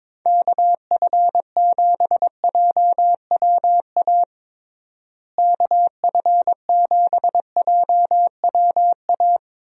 Morse code, heard as a keyed tone, about 22 wpm, 700 Hz. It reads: KF7JWA KF7JWA
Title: USA Call Signs - 6 Characters - Repeated 2x - 22wpm